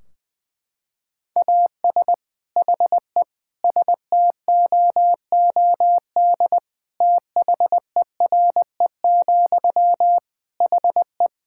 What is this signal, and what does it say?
Morse code, heard as a keyed tone, about 21 wpm, 700 Hz. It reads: AS HE STOOD THERE, HE
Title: Sentences from Top 1600 Words 20wpm